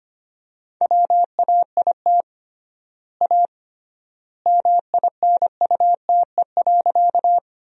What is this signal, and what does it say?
Morse code, heard as a keyed tone, about 25 wpm, 700 Hz. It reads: WAIT A MINUTE.
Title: 3x Word-Spacing - Sentences from Top 500 Words - 25wpm